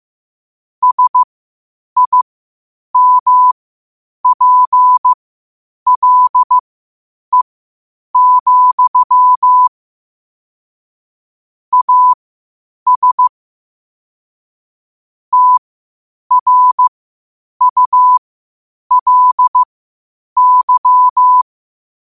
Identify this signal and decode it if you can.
Morse code, heard as a keyed tone, about 15 wpm, spaced as for 8 wpm, 1000 Hz. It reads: SIMPLE, AS TRULY